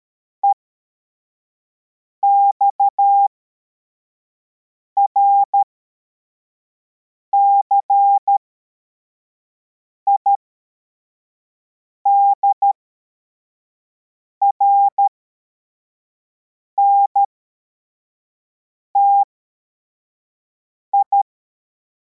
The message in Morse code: EXRCI DRNTI